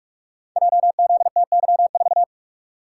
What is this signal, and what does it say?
Morse code, heard as a keyed tone, about 44 wpm, 700 Hz. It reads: JZTY4
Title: Code Groups - 5 Letter:Numbers - 45wpm